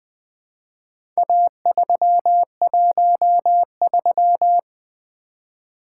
Morse code, heard as a keyed tone, about 20 wpm, 700 Hz. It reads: A313
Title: Four Letter-Number Combinations - 20wpm